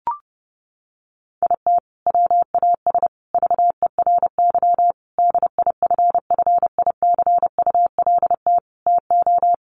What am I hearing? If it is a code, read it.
Morse code, heard as a keyed tone, about 30 wpm, 700 Hz. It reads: IT WAS VERY DIFFICULT TO
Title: Sentences from Top 1100 Words 30wpm